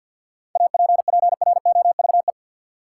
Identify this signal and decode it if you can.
Morse code, heard as a keyed tone, about 50 wpm, 700 Hz. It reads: APPROVE
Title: Seven Letter Words - 50wpm